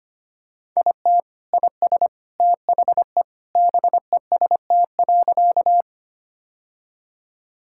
Morse code, heard as a keyed tone, about 25 wpm, 700 Hz. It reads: IT IS THE BEST.
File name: practice / 25wpm / Sentences with 3-Word Phrases - 25wpm